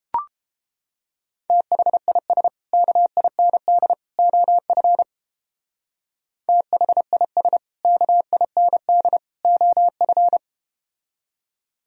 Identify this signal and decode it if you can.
Morse code, heard as a keyed tone, about 29 wpm, 700 Hz. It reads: THIS KIND OF THIS KIND OF